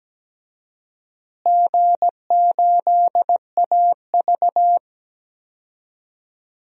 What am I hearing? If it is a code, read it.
Morse code, heard as a keyed tone, about 17 wpm, 700 Hz. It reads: G8AV
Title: DXCC-100 Call Signs - 4 Characters - Repeated 3x - 17wpm